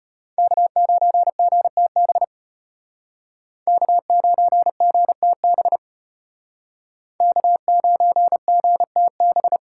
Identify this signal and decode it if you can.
Morse code, heard as a keyed tone, about 32 wpm, 700 Hz. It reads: K9GTB K9GTB K9GTB